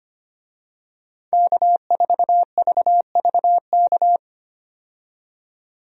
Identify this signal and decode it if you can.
Morse code, heard as a keyed tone, about 25 wpm, 700 Hz. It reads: K4VVK
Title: USA Call Signs - 5 Characters - 25wpm